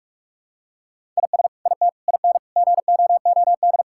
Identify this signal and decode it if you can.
Morse code, heard as a keyed tone, about 41 wpm, 700 Hz. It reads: IS IT IN GOOD
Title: Sentences from Top 1000 Words - Review Part 6 - 45wpm